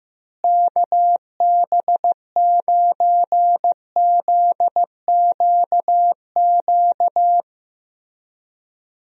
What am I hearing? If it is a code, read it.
Morse code, heard as a keyed tone, about 15 wpm, 700 Hz. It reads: KB9ZQQ
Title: USA Call Signs - 6 Characters - Repeated 2x - 15wpm